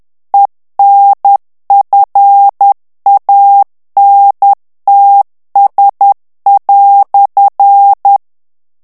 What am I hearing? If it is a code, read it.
Morse code, heard as a keyed tone, about 10 wpm, 800 Hz. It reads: ENFANTS"
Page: Français - texte en clair 1 - vitesse : 10 mots minute